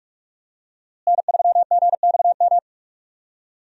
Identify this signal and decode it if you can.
Morse code, heard as a keyed tone, about 43 wpm, 700 Hz. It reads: N3GXM
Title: USA Call Signs - 5 Characters - Repeated 2x - 45wpm